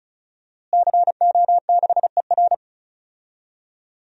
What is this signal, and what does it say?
Morse code, heard as a keyed tone, about 36 wpm, 700 Hz. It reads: CO6ER
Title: DXCC-100 Call Signs - 5 Characters - Repeated 3x - 35wpm